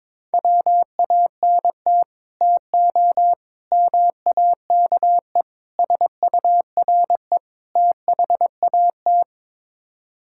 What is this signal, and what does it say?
Morse code, heard as a keyed tone, about 21 wpm, 700 Hz. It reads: WANT TO MAKE SURE THAT
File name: practice / 22wpm / Build A Phrase - 5-Word Phrases - 22wpm